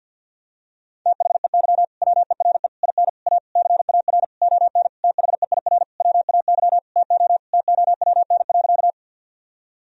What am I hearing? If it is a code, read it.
Morse code, heard as a keyed tone, about 51 wpm, 700 Hz. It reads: THEY WERE IN A CAR ON THEIR WAY TO TOWN.
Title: Sentences from Top 1000 Words - Review Part 1 - 50wpm